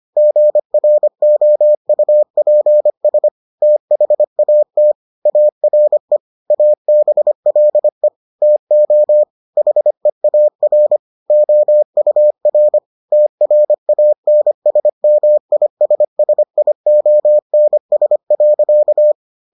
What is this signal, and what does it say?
Morse code, heard as a keyed tone, about 25 wpm, 600 Hz. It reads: GROUPS THAT ARE ABLE TO HEAR OUR TRANSMISSIONS.